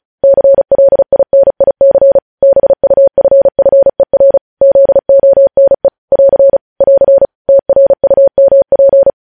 Morse code, heard as a keyed tone, about 36 wpm, 550 Hz. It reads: CLINIC BUFFER ZONE <AR> <AR> TRUMP
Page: news headlines in morse code at 35 wpm